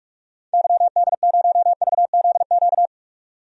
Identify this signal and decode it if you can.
Morse code, heard as a keyed tone, about 44 wpm, 700 Hz. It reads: YD0VZQ
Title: DXCC-100 Call Signs - 6 Characters - 45wpm